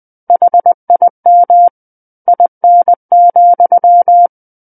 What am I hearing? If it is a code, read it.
Morse code, heard as a keyed tone, about 20 wpm, 700 Hz. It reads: HIM IN,
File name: 20 WPM